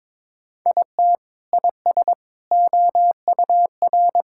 Morse code, heard as a keyed tone, about 22 wpm, 700 Hz. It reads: IT IS OUR
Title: Sentences from Top 1600 Words 22wpm